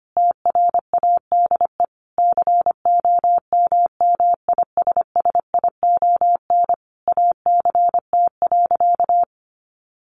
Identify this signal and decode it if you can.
Morse code, heard as a keyed tone, about 25 wpm, 700 Hz. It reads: TRADE COMMISSION ACT.
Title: Sentences from Top 1400 Words 25wpm